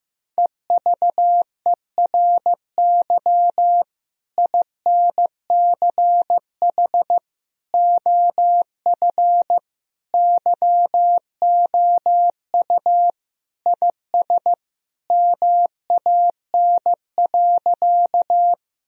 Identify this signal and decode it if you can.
Morse code, heard as a keyed tone, about 15 wpm, 700 Hz. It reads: EVERY INCH OF YOU IS MAN.
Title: Sentences from Top 1000 Words - Review Part 3 - 15wpm